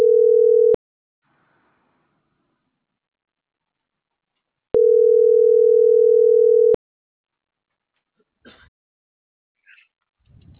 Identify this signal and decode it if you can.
Ringback tone (North America), 440 + 480 Hz.